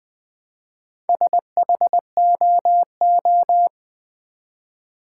Morse code, heard as a keyed tone, about 20 wpm, 700 Hz. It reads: SHOO
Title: Build A Word - 2 Letters At A Time - 8-Letter Words - 20wpm